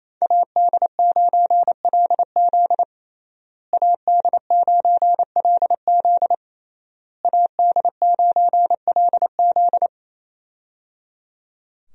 Morse code, heard as a keyed tone, about 27 wpm, 700 Hz. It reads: AD9LZ AD9LZ AD9LZ
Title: USA Call Signs - 5 Characters - Repeated 3x - 28wpm